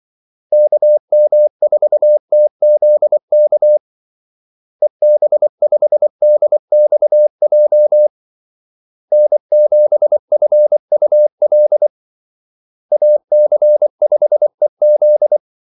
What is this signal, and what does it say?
Morse code, heard as a keyed tone, about 24 wpm, 600 Hz. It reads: KM4TZK EB5DXJ N7FUL AC5EZ
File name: Callsigns-24wpm